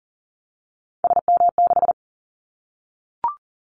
Morse code, heard as a keyed tone, about 40 wpm, 700 Hz. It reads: SM6
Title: Callsign Prefixes for Top 25 DXCC - 40wpm